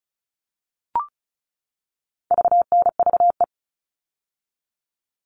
Morse code, heard as a keyed tone, about 34 wpm, 700 Hz. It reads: VNVE
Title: All Cut Numbers - 4-Digit 35wpm